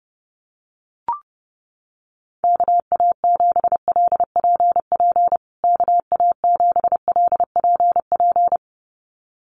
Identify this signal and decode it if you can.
Morse code, heard as a keyed tone, about 30 wpm, 700 Hz. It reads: KA7LPP KA7LPP
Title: USA Call Signs Repeated 2x 30wpm